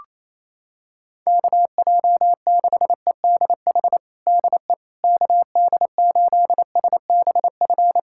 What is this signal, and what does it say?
Morse code, heard as a keyed tone, about 28 wpm, 700 Hz. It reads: KJ6EDH DE KD8SBF
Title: USA Callsign DE Callsign - 6 Characters - 28wpm